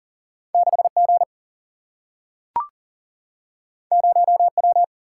Morse code, heard as a keyed tone, about 40 wpm, 700 Hz. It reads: 6G 0W
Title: Single-Letter Single-Number 40wpm